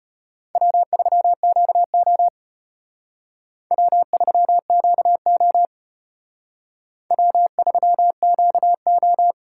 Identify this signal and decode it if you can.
Morse code, heard as a keyed tone, about 33 wpm, 700 Hz. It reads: W3QO W3QO W3QO